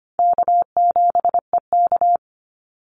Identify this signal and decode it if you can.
Morse code, heard as a keyed tone, about 25 wpm, 700 Hz. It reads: K7EK
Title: Easy Call Signs Repeated 3x - 25wpm